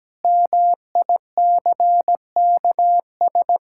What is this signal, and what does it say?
Morse code, heard as a keyed tone, about 17 wpm, 700 Hz. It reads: MICKS
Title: Pangram Challenge v2 - 17wpm